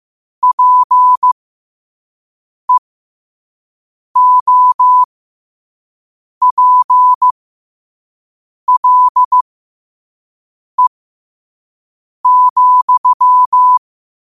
Morse code, heard as a keyed tone, about 15 wpm, spaced as for 5 wpm, 1000 Hz. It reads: PEOPLE,
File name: 5wpm